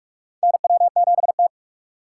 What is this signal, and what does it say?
Morse code, heard as a keyed tone, about 46 wpm, 700 Hz. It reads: NW7T
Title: USA Call Signs - 2x Character Spacing - 45wpm